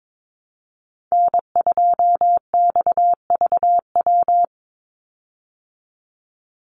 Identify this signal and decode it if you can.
Morse code, heard as a keyed tone, about 22 wpm, 700 Hz. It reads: N2XVW
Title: USA Call Signs 22wpm